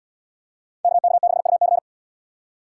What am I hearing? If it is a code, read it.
Morse code, heard as a keyed tone, about 77 wpm, 700 Hz. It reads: KK6UC